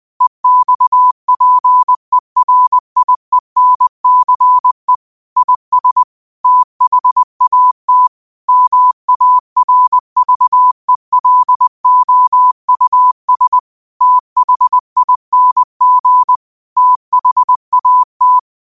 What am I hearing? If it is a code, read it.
Morse code, heard as a keyed tone, about 20 wpm, 1000 Hz. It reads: EXPERIENCE IS THAT MARVELOUS THING THAT